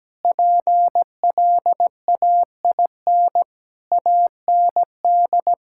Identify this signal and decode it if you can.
Morse code, heard as a keyed tone, about 17 wpm, 700 Hz. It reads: PLAIN AND
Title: Sentences from Top 1000 Words - Review Part 5 - 17wpm